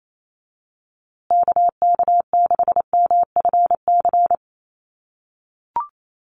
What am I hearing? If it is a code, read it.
Morse code, heard as a keyed tone, about 28 wpm, 700 Hz. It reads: KK6MFC
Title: USA Call Signs Repeated 2x 28wpm